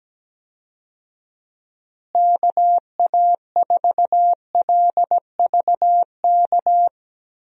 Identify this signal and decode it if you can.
Morse code, heard as a keyed tone, about 17 wpm, 700 Hz. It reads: KA4LVK
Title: USA Call Signs - 6 Characters - Repeated 3x - 17wpm